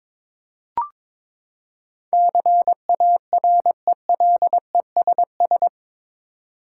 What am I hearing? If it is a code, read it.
Morse code, heard as a keyed tone, about 22 wpm, 700 Hz. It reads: CARELESS
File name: practice / 22wpm / Eight Letter Words - 22wpm